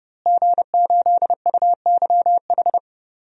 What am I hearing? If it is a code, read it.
Morse code, heard as a keyed tone, about 29 wpm, 700 Hz. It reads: G8UYH
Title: DXCC-100 Call Signs - 5 Characters - Repeated 3x - 30wpm